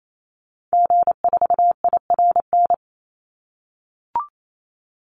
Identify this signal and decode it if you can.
Morse code, heard as a keyed tone, about 28 wpm, 700 Hz. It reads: G4IRN
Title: Call Signs - 28wpm